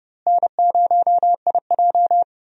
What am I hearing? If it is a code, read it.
Morse code, heard as a keyed tone, about 30 wpm, 700 Hz. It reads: N0IJ
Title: USA Call Signs - 4 Characters - Repeated 3x - 30wpm